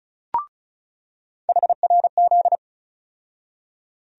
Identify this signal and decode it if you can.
Morse code, heard as a keyed tone, about 36 wpm, 700 Hz. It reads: HRZ
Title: USA Call Sign Postfixes - 3 Letters 35wpm